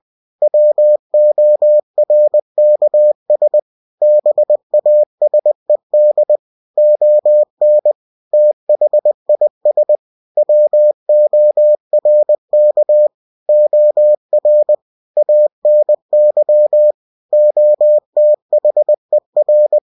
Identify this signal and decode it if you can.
Morse code, heard as a keyed tone, about 20 wpm, 600 Hz. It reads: WORKS BASED ON THIS WORK OR ANY OTHER
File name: cw-books / winnie-the-pooh / winnie-20wpm-page-0114